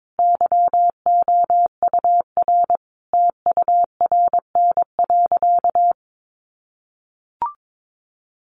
Morse code, heard as a keyed tone, about 22 wpm, 700 Hz. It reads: YOUR TURN.
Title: Sentences from Top 600 Words 22wpm